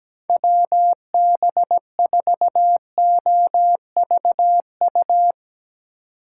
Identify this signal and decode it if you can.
Morse code, heard as a keyed tone, about 17 wpm, 700 Hz. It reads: WB4OVU
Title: Speed Racing - USA 2x3 Call Signs 17wpm